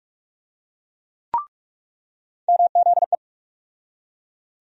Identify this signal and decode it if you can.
Morse code, heard as a keyed tone, about 42 wpm, 700 Hz. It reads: MZE
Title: USA Call Sign Postfixes - 3 Letters 45wpm